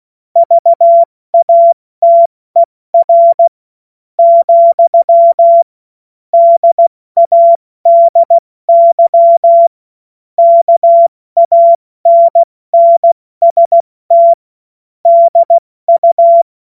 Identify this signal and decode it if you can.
Morse code, heard as a keyed tone, about 16 wpm, 680 Hz. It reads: VATER , DADY KANNST DU
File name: Morsekurs / Witze / Witz 10. August 2021, 70 F80